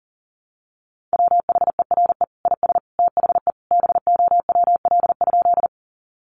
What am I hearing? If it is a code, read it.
Morse code, heard as a keyed tone, about 40 wpm, 700 Hz. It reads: WHERE IS THE BOWL?
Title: Sentences from Top 1200 Words 40wpm